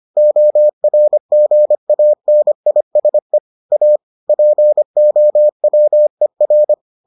Morse code, heard as a keyed tone, about 25 wpm, 600 Hz. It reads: ORGANISE A POWER